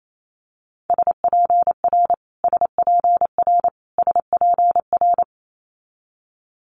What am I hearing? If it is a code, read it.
Morse code, heard as a keyed tone, about 28 wpm, 700 Hz. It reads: SPR SPR SPR
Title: Multi-letter Phonemes Repeated 3x 28wpm